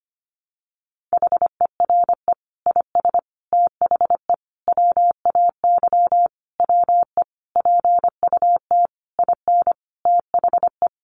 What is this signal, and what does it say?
Morse code, heard as a keyed tone, about 25 wpm, 700 Hz. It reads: HERE IS THE WAY WE PUT IN THE
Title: Sentences from Top 200 Words 25wpm